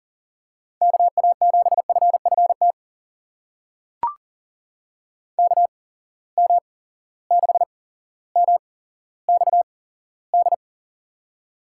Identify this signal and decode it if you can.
Morse code, heard as a keyed tone, about 42 wpm, spaced as for 17 wpm, 700 Hz. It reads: KA7FFT KM6MXD
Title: USA Call Signs - 6 Characters - 8x Character Spacing - 40wpm